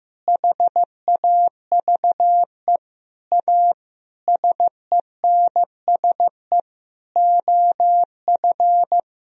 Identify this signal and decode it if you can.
Morse code, heard as a keyed tone, about 16 wpm, 700 Hz. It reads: HAVE A SENSE OF
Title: Sentences from Top 1000 Words - Review Part 4 - 15wpm